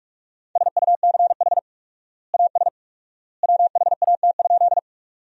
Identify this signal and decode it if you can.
Morse code, heard as a keyed tone, about 41 wpm, 700 Hz. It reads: SUCH AS WHAT?
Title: 4x Word-Spacing - Sentences from Top 200 Words - 45wpm